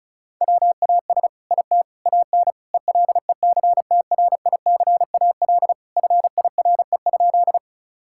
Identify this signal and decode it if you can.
Morse code, heard as a keyed tone, about 36 wpm, 700 Hz. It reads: WAS IT AN ELECTRICAL FIRE?
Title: Sentences from Top 1000 Words - Review Part 6 - 35wpm